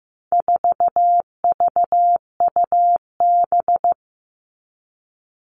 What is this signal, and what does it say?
Morse code, heard as a keyed tone, about 15 wpm, 700 Hz. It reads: 4VUB
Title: All Cut Numbers - 4-Digit 15wpm